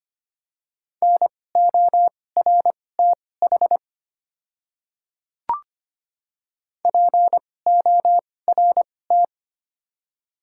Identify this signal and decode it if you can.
Morse code, heard as a keyed tone, about 25 wpm, spaced as for 18 wpm, 700 Hz. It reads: NORTH PORT
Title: 2x Character Spacing - Top 300 Words - 25wpm